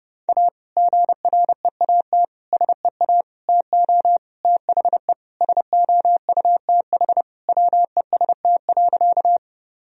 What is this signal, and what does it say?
Morse code, heard as a keyed tone, about 30 wpm, 700 Hz. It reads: A GREAT SEA TO THE SOUTH WEST.
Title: Sentences from Top 1000 Words - Review Part 2 - 30wpm